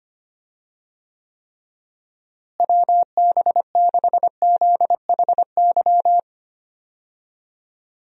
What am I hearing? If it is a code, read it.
Morse code, heard as a keyed tone, about 25 wpm, 700 Hz. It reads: WB6ZHY
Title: USA Call Signs - 6 Characters - Repeated 3x - 25wpm